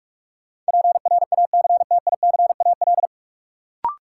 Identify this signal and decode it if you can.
Morse code, heard as a keyed tone, about 45 wpm, 700 Hz. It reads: PRACTICAL
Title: Sentences from Top 1400 Words - New Word Drill Alt - 45wpm